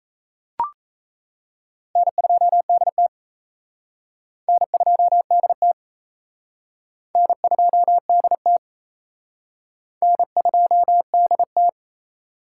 Morse code, heard as a keyed tone, about 35 wpm, 700 Hz. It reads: N2DT N2DT N2DT N2DT
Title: Speed Racing - USA Call Signs 28wpm